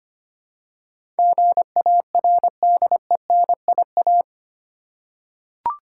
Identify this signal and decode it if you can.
Morse code, heard as a keyed tone, about 25 wpm, 700 Hz. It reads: GARDENIA
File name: practice / 25wpm / Difficult Words with Hints - 25wpm